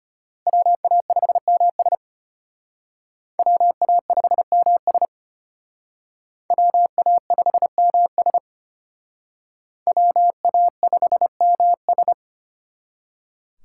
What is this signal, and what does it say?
Morse code, heard as a keyed tone, about 31 wpm, 700 Hz. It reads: WA5MS WA5MS WA5MS WA5MS
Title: Speed Racing - USA 2x2 Call Signs 25wpm